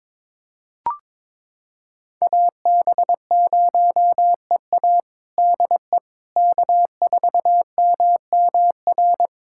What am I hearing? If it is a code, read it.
Morse code, heard as a keyed tone, about 21 wpm, 700 Hz. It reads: AB0EA DE K4MMR
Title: USA Callsign DE Callsign - 5 Characters - 22wpm